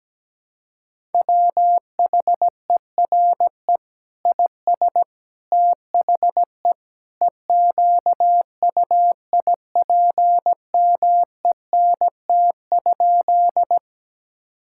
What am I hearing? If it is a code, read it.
Morse code, heard as a keyed tone, about 17 wpm, 700 Hz. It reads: WHERE IS THE EQUIPMENT?
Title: Sentences from Top 1000 Words - Review Part 5 - 17wpm